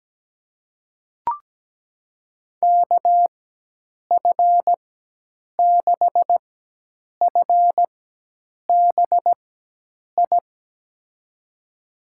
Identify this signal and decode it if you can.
Morse code, heard as a keyed tone, about 17 wpm, spaced as for 8 wpm, 700 Hz. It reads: KF6FBI
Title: USA Call Signs - 6 Characters - 4x Character Spacing - 17wpm